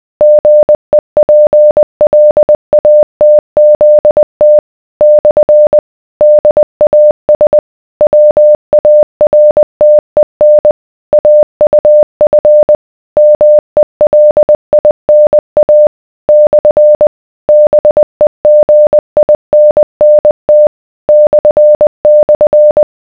Morse code, heard as a keyed tone, about 20 wpm, 600 Hz. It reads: GEPLATZT / DAS WARTEN AUF MELINA / BEGINNT //